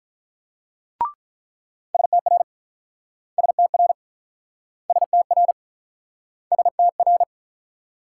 Morse code, heard as a keyed tone, about 48 wpm, 700 Hz. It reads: STR STR STR STR